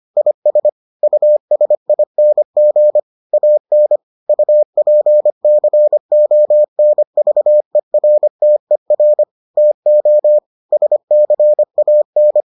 Morse code, heard as a keyed tone, about 25 wpm, 600 Hz. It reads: IS USING AN UPCONVERTER TO SCAN